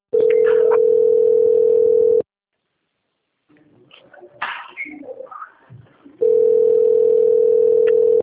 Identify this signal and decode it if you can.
Ringback tone (North America), 440 + 480 Hz.
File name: out- - - - - .44235